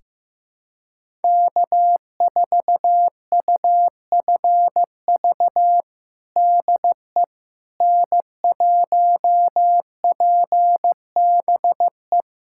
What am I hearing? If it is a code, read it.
Morse code, heard as a keyed tone, about 15 wpm, 700 Hz. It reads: K4UFV DE N1PBE
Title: USA Callsign DE Callsign - 5 Characters - 15wpm